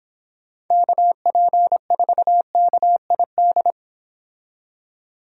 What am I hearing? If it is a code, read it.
Morse code, heard as a keyed tone, about 26 wpm, 700 Hz. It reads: KP4KID